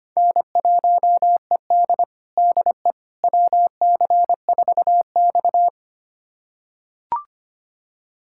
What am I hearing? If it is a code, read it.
Morse code, heard as a keyed tone, about 25 wpm, 700 Hz. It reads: N1ED DE WC4X
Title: USA Callsign DE Callsign - 4 Characters - 25wpm